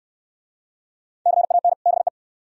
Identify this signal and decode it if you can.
Morse code, heard as a keyed tone, about 64 wpm, 700 Hz. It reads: CAN BE